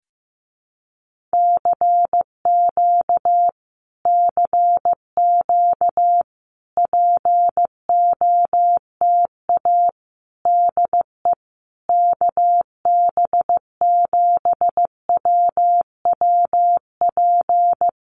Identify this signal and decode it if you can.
Morse code, heard as a keyed tone, about 15 wpm, 700 Hz. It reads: CQ CQ POTA DE KB7WWP
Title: Common QSO Phrases Encore 15wpm